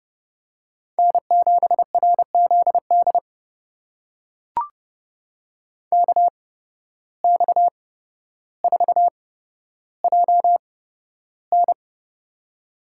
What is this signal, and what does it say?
Morse code, heard as a keyed tone, about 30 wpm, spaced as for 13 wpm, 700 Hz. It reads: N7RZD KX4JN
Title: USA Call Signs - 5 Characters - 8x Character Spacing - 30wpm